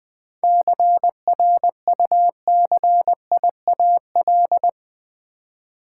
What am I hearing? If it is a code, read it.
Morse code, heard as a keyed tone, about 20 wpm, 700 Hz. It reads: CRUCIAL
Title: 7 Letter Words - First 3 Letters Given - 20wpm